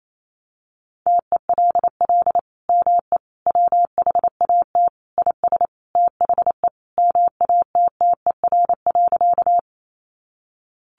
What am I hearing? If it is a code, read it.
Morse code, heard as a keyed tone, about 28 wpm, 700 Hz. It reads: TELL ME WHAT IS THE MATTER.
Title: Sentences from Top 700 Words 28wpm